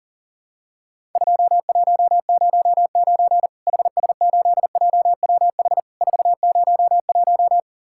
Morse code, heard as a keyed tone, about 40 wpm, 700 Hz. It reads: 2109 HS8JWH 401
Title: Worked All Europe DX Contest - QTC Exchange - 40wpm